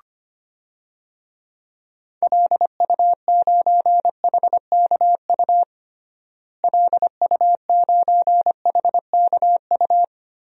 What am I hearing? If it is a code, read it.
Morse code, heard as a keyed tone, about 25 wpm, 700 Hz. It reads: LU9HKU LU9HKU
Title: DXCC-100 Call Signs - 6 Characters - Repeated 3x - 25wpm